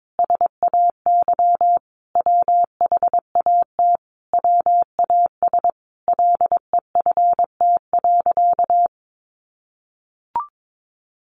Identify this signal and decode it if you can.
Morse code, heard as a keyed tone, about 22 wpm, 700 Hz. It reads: SAY WHAT WAS LEFT.
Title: Sentences from Top 1200 Words 22wpm